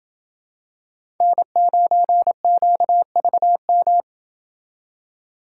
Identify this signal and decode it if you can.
Morse code, heard as a keyed tone, about 27 wpm, 700 Hz. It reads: N9QVM